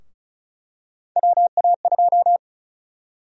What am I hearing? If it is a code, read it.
Morse code, heard as a keyed tone, about 36 wpm, 700 Hz. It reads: WA2
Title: USA Call Sign Prefixes - 35wpm